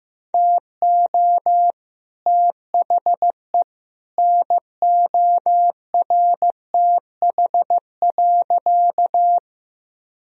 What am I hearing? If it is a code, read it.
Morse code, heard as a keyed tone, about 15 wpm, 700 Hz. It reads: TO THE NORTH.
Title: Sentences from Top 1000 Words - Review Part 1 - 15wpm